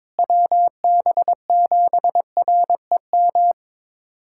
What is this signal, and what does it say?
Morse code, heard as a keyed tone, about 21 wpm, 700 Hz. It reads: WB7REM
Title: USA Callsign DE Callsign - 6 Characters - 22wpm